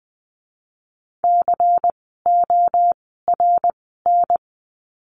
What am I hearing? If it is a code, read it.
Morse code, heard as a keyed tone, about 20 wpm, spaced as for 14 wpm, 700 Hz. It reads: CORN
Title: 094 Lesson 16f - Letter c - Words 20wpm